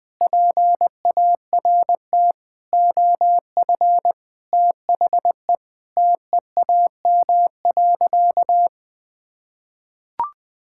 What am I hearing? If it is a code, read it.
Morse code, heard as a keyed tone, about 20 wpm, 700 Hz. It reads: PART OF THE TEAM.
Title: Sentences with 4-Word Phrases - 20wpm